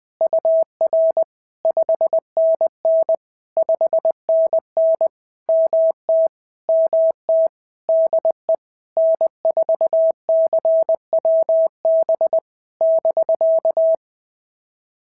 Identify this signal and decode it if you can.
Morse code, heard as a keyed tone, about 19 wpm, 650 Hz. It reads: UR 5NN 5NN MT MT DE N4CWB <BK>
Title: POTA QSO Practice - 20wpm